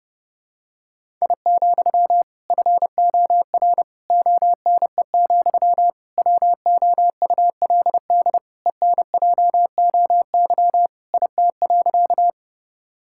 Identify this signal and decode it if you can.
Morse code, heard as a keyed tone, about 30 wpm, 700 Hz. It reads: I, FOR ONE, WOULD ENJOY IT.
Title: Sentences from Top 1000 Words - Review Part 5 - 30wpm